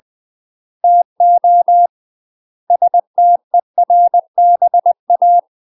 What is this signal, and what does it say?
Morse code, heard as a keyed tone, about 20 wpm, 700 Hz. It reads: TO STERBA
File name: UK MASTER QSOs TRAINING / 20WPM / UK MASTER QSOs 0001